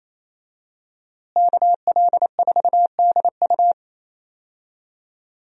Morse code, heard as a keyed tone, about 28 wpm, 700 Hz. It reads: KL4DU
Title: USA Call Signs - 5 Characters - Repeated 3x - 28wpm